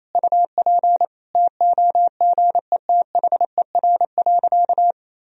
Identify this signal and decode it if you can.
Morse code, heard as a keyed tone, about 28 wpm, 700 Hz. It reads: UP TOGETHER.
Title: Sentences from Top 1000 Words - Review Part 1 - 28wpm